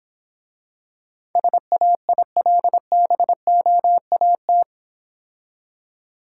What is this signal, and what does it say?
Morse code, heard as a keyed tone, about 26 wpm, 700 Hz. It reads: SAILBOAT